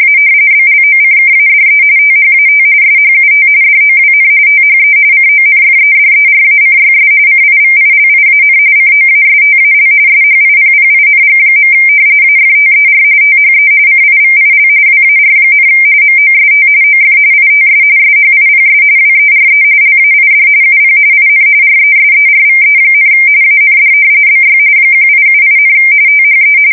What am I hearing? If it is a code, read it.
RTTY, about 45.45 baud. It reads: A BLOCKADE THAT SHUT DOWN THE CONFEDERACYS CIVILIAN COASTAL SHIPPING. AFTER THE CIVIL WAR, MOST OF ITS SHIPS WERE LAID UP IN RESERVE, AND BY 1878, THE
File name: fsk170 / usnavy1